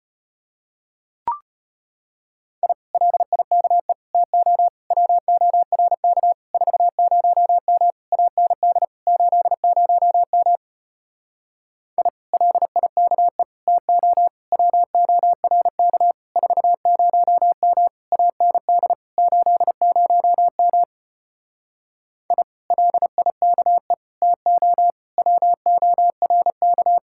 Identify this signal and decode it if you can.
Morse code, heard as a keyed tone, about 33 wpm, 700 Hz. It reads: I LIKE TO WORK 40M AND 80M I LIKE TO WORK 40M AND 80M I LIKE TO WORK